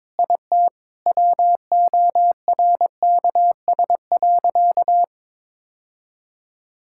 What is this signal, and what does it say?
Morse code, heard as a keyed tone, about 22 wpm, 700 Hz. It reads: IT WORKS.
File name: practice / 22wpm / Sentences with 3-Word Phrases - 22wpm